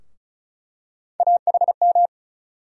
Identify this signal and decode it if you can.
Morse code, heard as a keyed tone, about 34 wpm, 700 Hz. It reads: AHM
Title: USA Call Sign Postfixes - 3 Letters 35wpm